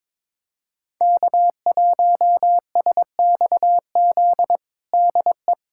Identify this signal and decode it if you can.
Morse code, heard as a keyed tone, about 22 wpm, 700 Hz. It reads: K1SXZ DE
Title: USA Callsign DE Callsign - 5 Characters - 22wpm